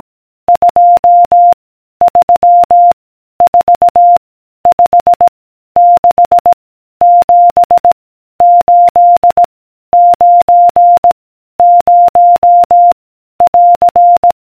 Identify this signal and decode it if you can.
Morse code, heard as a keyed tone, about 18 wpm, 700 Hz. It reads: 2 3 4 5 6 7 8 9 0 <AR>